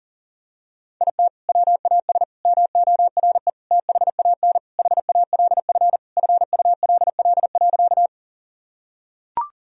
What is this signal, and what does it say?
Morse code, heard as a keyed tone, about 40 wpm, 700 Hz. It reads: IT WAS MORE THAN HALF FULL.
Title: Sentences from Top 1000 Words - Review Part 3 - 40wpm